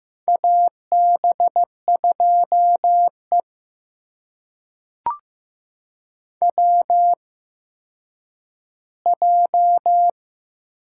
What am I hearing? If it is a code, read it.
Morse code, heard as a keyed tone, about 15 wpm, spaced as for 9 wpm, 700 Hz. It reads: AB2E WJ
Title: USA Call Signs - 4 Characters - 8x Character Spacing - 15wpm